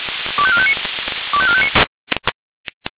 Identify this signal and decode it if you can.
Five-tone selective call (ZVEI), heard as quick runs of five tones, tones of about 70 ms. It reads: 25459 25459